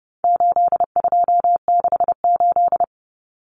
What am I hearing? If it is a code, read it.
Morse code, heard as a keyed tone, about 30 wpm, 700 Hz. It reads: 8268
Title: Numbers - 4-Digit 30wpm